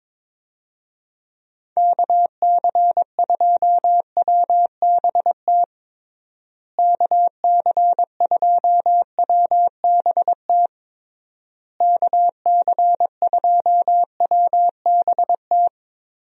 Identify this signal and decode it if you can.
Morse code, heard as a keyed tone, about 22 wpm, 700 Hz. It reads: KC2WBT KC2WBT KC2WBT
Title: USA Call Signs - 6 Characters - Repeated 3x - 22wpm